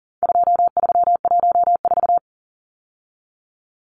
Morse code, heard as a keyed tone, about 40 wpm, 700 Hz. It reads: 2314
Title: Numbers - 4-Digit 40wpm